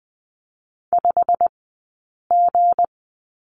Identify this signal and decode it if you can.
Morse code, heard as a keyed tone, about 20 wpm, spaced as for 8 wpm, 700 Hz. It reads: 5G
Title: 165 Lesson 27d - Number 7 - Two Characters 20wpm